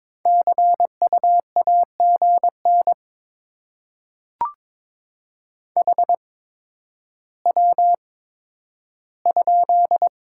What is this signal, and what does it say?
Morse code, heard as a keyed tone, about 22 wpm, spaced as for 11 wpm, 700 Hz. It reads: CU AGN HW?
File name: Top 20 QSO Elements - 8x Character Spacing - 22wpm